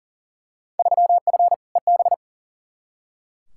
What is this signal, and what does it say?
Morse code, heard as a keyed tone, about 37 wpm, 700 Hz. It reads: 3F EB
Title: Field Day Exchanges - 40wpm